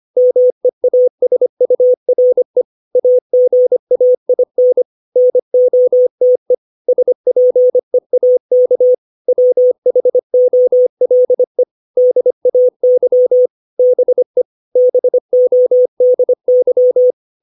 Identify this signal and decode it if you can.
Morse code, heard as a keyed tone, about 25 wpm, 500 Hz. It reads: MEASURE AGAIN NOTE SPEAK WHOLE DAY BE BODY
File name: 632 words / 632 Words 25WPM